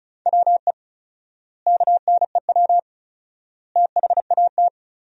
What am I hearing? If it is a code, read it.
Morse code, heard as a keyed tone, about 34 wpm, 700 Hz. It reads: WE KNEW THAT
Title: 4x Word-Spacing - Sentences from Top 400 Words - 35wpm